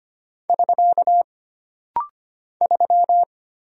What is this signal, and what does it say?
Morse code, heard as a keyed tone, about 25 wpm, 700 Hz. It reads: <SK> 3
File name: Single Character (Letter:Number:Punctuation:Prosign) - Rapid-Fire - 25wpm